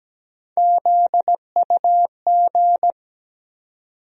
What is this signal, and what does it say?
Morse code, heard as a keyed tone, about 17 wpm, 700 Hz. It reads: ZUG
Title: USA Call Sign Postfixes - 3 Letters 17wpm